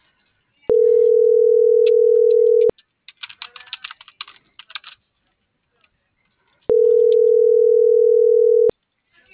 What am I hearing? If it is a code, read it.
Ringback tone (North America), 440 + 480 Hz.